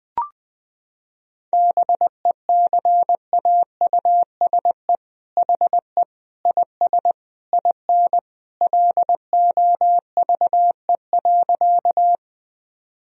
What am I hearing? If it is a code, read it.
Morse code, heard as a keyed tone, about 20 wpm, 700 Hz. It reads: BECAUSE HE IS IN LOVE.
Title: Sentences from Top 1000 Words - Review Part 3 - 20wpm